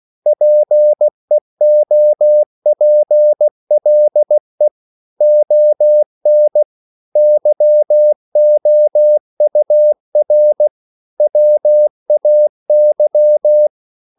Morse code, heard as a keyed tone, about 16 wpm, 600 Hz. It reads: PEOPLE ON YOUR WAY
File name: CW / Practice / EnglishQuotations0216